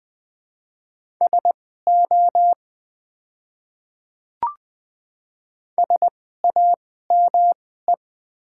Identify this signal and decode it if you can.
Morse code, heard as a keyed tone, about 20 wpm, spaced as for 15 wpm, 700 Hz. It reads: SO SAME
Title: 2x Character Spacing - Top 200 Words - 20wpm